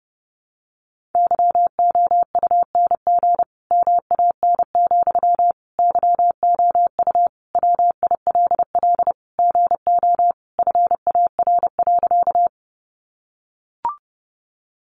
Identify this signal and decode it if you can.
Morse code, heard as a keyed tone, about 30 wpm, 700 Hz. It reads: YOUNG MAN, YOU WILL GO FAR.
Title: Sentences from Top 400 Words 30wpm